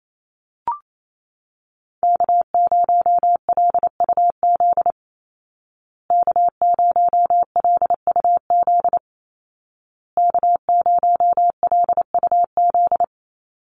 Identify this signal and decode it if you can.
Morse code, heard as a keyed tone, about 28 wpm, 700 Hz. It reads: K0LUZ K0LUZ K0LUZ
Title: Call Signs Repeated 2x - 28wpm